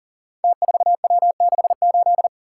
Morse code, heard as a keyed tone, about 40 wpm, 700 Hz. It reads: T4W68
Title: Code Groups - 5 Letter:Numbers - 40wpm